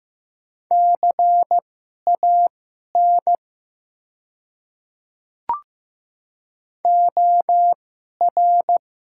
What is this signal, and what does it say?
Morse code, heard as a keyed tone, about 15 wpm, spaced as for 11 wpm, 700 Hz. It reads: CAN OR
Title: 2x Character Spacing - Top 100 Words - 15wpm